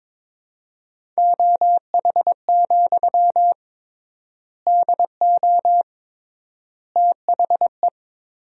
Morse code, heard as a keyed tone, about 22 wpm, 700 Hz. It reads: OH, DO THE
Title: 3x Word-Spacing - Sentences from Top 500 Words - 22wpm